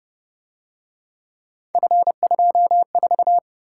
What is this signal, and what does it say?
Morse code, heard as a keyed tone, about 29 wpm, 700 Hz. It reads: F24
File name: Speed-Racing - Three Letter-Number Combinations - 25wpm